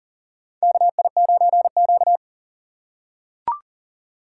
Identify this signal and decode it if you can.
Morse code, heard as a keyed tone, about 40 wpm, 700 Hz. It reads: KI9Q
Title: USA Call Signs - 4 Characters - 6x Character Spacing - 40wpm